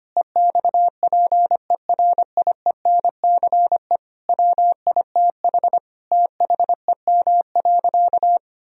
Morse code, heard as a keyed tone, about 25 wpm, 700 Hz. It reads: EXPERIENCE WITH THEM.
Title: Sentences from Top 1000 Words - Review Part 3 - 25wpm